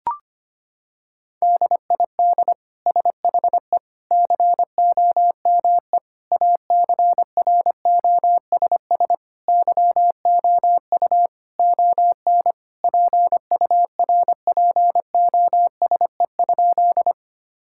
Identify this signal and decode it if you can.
Morse code, heard as a keyed tone, about 25 wpm, 700 Hz. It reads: DID SHE COME ACROSS YOU ON PURPOSE?
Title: Sentences from Top 1000 Words - Review Part 5 - 25wpm